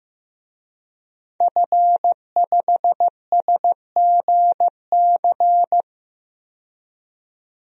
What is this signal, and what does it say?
Morse code, heard as a keyed tone, about 15 wpm, 700 Hz. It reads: F5SGC
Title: DXCC-100 Call Signs - 5 Characters - 15wpm